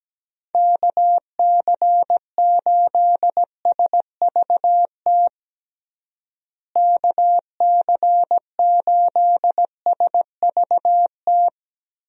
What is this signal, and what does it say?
Morse code, heard as a keyed tone, about 17 wpm, 700 Hz. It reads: KC8SVT KC8SVT
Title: USA Call Signs - 6 Characters - Repeated 3x - 17wpm